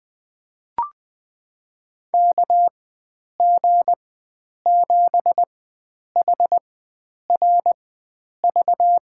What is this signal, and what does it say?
Morse code, heard as a keyed tone, about 20 wpm, spaced as for 9 wpm, 700 Hz. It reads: KG7HRV
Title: USA Call Signs - 6 Characters - 4x Character Spacing - 20wpm